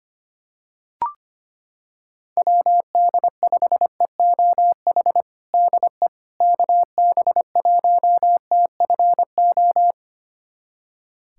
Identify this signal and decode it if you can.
Morse code, heard as a keyed tone, about 25 wpm, 700 Hz. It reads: WD5EOH DE KB1TFO
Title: USA Callsign DE Callsign - 6 Characters - 25wpm